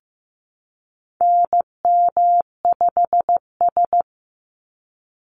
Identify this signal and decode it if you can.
Morse code, heard as a keyed tone, about 15 wpm, 700 Hz. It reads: NM5S
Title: NA SOTA Chaser Callsigns 15wpm